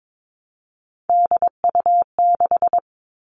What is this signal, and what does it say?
Morse code, heard as a keyed tone, about 22 wpm, 700 Hz. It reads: DU6
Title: All Cut Numbers - 3-Digit 22wpm